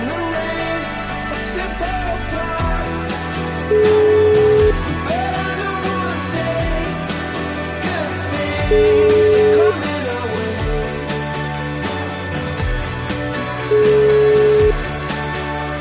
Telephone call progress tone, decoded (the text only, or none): ringback tone